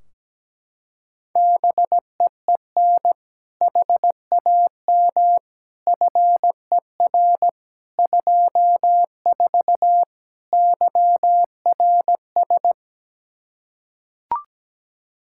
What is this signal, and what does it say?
Morse code, heard as a keyed tone, about 17 wpm, 700 Hz. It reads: BEEN HAM FER 24 YRS
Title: Practice QSO - Exchange 3 - More Info - 17wpm